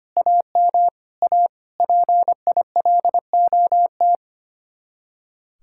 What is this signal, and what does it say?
Morse code, heard as a keyed tone, about 31 wpm, 700 Hz. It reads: AM A PILOT
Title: Speed-Racing Common QSO Phrases - 25wpm